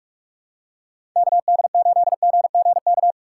Morse code, heard as a keyed tone, about 42 wpm, 700 Hz. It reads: KD8GGK
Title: USA Call Signs - 6 Characters - 4x Character Spacing - 45wpm